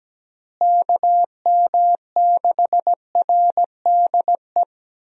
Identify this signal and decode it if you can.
Morse code, heard as a keyed tone, about 17 wpm, 700 Hz. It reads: KM6RDE
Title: USA Call Signs - 6 Characters - Repeated 2x - 17wpm